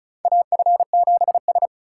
Morse code, heard as a keyed tone, about 34 wpm, 700 Hz. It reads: AF7S
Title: USA Call Signs - 4 Characters - 6x Character Spacing - 35wpm